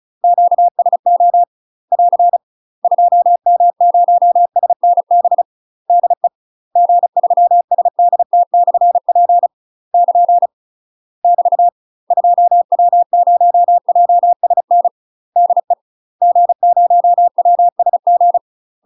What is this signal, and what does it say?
Morse code, heard as a keyed tone, about 34 wpm, 700 Hz. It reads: QSO <AR> 2M0SNB DE G3SDT/P <KN> <BT> 2W0JSN DE G0WSG
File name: UK MASTER QSOs TRAINING / 35WPM / UK MASTER QSOs 0032